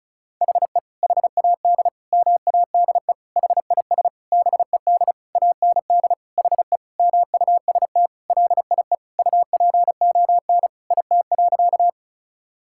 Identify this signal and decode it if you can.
Morse code, heard as a keyed tone, about 36 wpm, 700 Hz. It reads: HE HAD MADE HIS BED AND HE MUST LIE UPON IT.
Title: Sentences from Top 1600 Words 35wpm